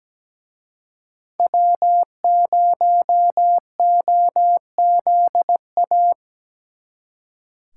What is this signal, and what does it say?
Morse code, heard as a keyed tone, about 19 wpm, 700 Hz. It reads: W0OZA
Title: Speed Racing - USA 1x3 Call Signs 17wpm